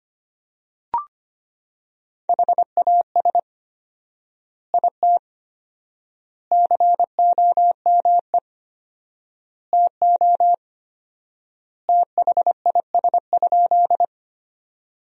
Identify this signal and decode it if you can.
Morse code, heard as a keyed tone, about 25 wpm, 700 Hz. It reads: HAS IT COME TO THIS?
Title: 4x Word-Spacing - Sentences from Top 100 Words - 25wpm